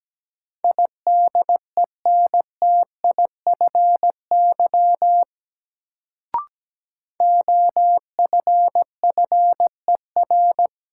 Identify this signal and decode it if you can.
Morse code, heard as a keyed tone, about 17 wpm, 700 Hz. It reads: IDENTIFY OFFER
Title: Sentences from Top 900 Words - New Word Drill - 17wpm